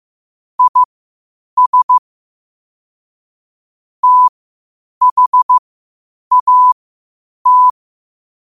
Morse code, heard as a keyed tone, about 15 wpm, spaced as for 8 wpm, 1000 Hz. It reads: IS THAT
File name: qotdfiles / 8wpm